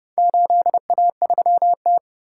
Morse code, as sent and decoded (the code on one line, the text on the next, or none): ---.. .- ...-- -
8A3T